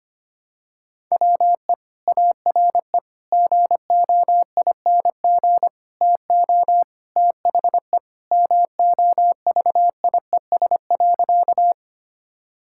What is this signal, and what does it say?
Morse code, heard as a keyed tone, about 25 wpm, 700 Hz. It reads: WE ARE GOING TO THE MOVIES.
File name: practice / 25wpm / Sentences with 4-Word Phrases - 25wpm